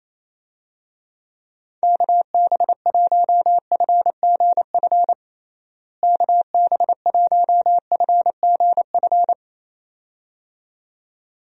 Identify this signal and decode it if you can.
Morse code, heard as a keyed tone, about 27 wpm, 700 Hz. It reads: KB1FGF KB1FGF
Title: USA Call Signs - 6 Characters - Repeated 2x - 28wpm